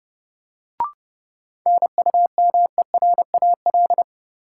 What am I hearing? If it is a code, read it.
Morse code, heard as a keyed tone, about 30 wpm, 700 Hz. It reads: NUMERAL
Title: Sentences from Top 1000 Words - Review Part 2 - New Word Drill - 30wpm